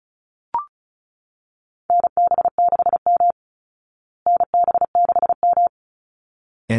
Morse code, heard as a keyed tone, about 34 wpm, 700 Hz. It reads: NB6M NB6M
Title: Call Signs Repeated 3x - 35wpm